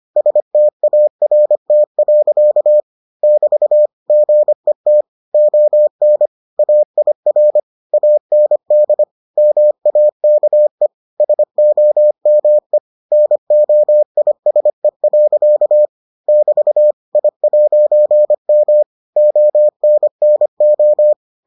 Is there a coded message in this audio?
Yes: Morse code from 0.2 to 21.1 s, heard as a keyed tone, about 25 wpm, 600 Hz. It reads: START. <BT> GET ON AIR AND MAKE SOME NOISE. <BT> I'M ONNO